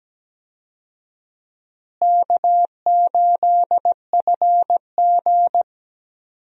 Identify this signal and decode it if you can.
Morse code, heard as a keyed tone, about 17 wpm, 700 Hz. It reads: K8FG
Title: USA Call Signs - 4 Characters - Repeated 3x - 17wpm